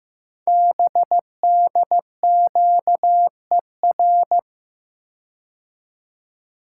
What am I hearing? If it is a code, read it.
Morse code, heard as a keyed tone, about 15 wpm, 700 Hz. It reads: BDQER
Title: Code Groups - 5 Letters - 15wpm